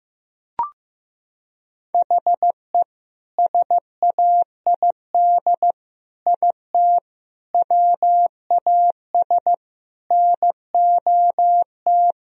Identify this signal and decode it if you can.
Morse code, heard as a keyed tone, about 15 wpm, 700 Hz. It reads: HE SAID IT WAS NOT
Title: Sentences from Top 1000 Words - Review Part 2 - 15wpm